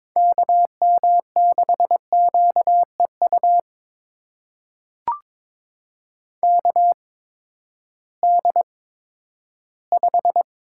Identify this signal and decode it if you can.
Morse code, heard as a keyed tone, about 22 wpm, spaced as for 13 wpm, 700 Hz. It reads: KM6QEU KD5
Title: USA Call Signs - 6 Characters - 8x Character Spacing - 22wpm